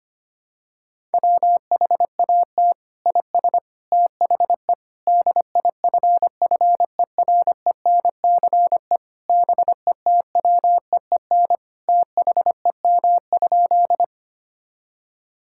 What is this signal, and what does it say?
Morse code, heard as a keyed tone, about 25 wpm, 700 Hz. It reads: WHAT IS THE DIFFERENCE BETWEEN THEM?
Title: Sentences from Top 1000 Words - Review Part 5 - 25wpm